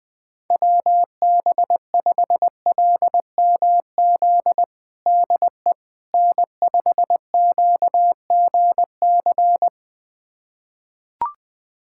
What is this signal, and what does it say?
Morse code, heard as a keyed tone, about 20 wpm, 700 Hz. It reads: WB5LMZ DE N5QGC
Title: USA Callsign DE Callsign 20wpm